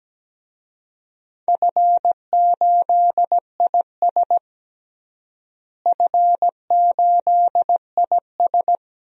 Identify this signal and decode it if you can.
Morse code, heard as a keyed tone, about 17 wpm, 700 Hz. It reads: F8IS F8IS
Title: DXCC-100 Call Signs - 4 Characters - Repeated 3x - 17wpm